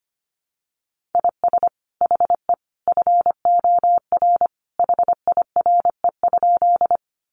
Morse code, heard as a keyed tone, about 25 wpm, 700 Hz. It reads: IS HE FOR HIRE?
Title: Sentences from Top 1400 Words 25wpm